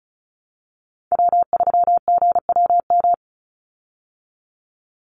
Morse code, heard as a keyed tone, about 34 wpm, 700 Hz. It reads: W3GWM
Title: Call Signs Repeated 2x - 35wpm